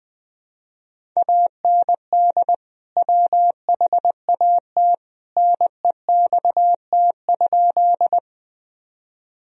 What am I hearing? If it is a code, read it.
Morse code, heard as a keyed tone, about 20 wpm, 700 Hz. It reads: AND WHAT NEXT?
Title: Sentences from Top 1000 Words - Review Part 1 - 20wpm